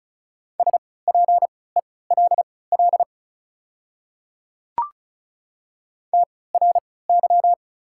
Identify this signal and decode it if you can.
Morse code, heard as a keyed tone, about 34 wpm, spaced as for 20 wpm, 700 Hz. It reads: SPELL TRY
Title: 3x Character Spacing - Top 500 Words - 35wpm